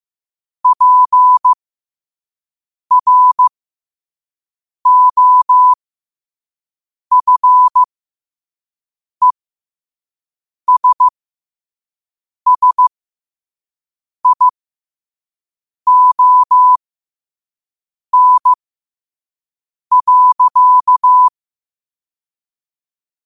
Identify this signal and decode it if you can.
Morse code, heard as a keyed tone, about 15 wpm, spaced as for 5 wpm, 1000 Hz. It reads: PROFESSION.
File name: qotdfiles / 5wpm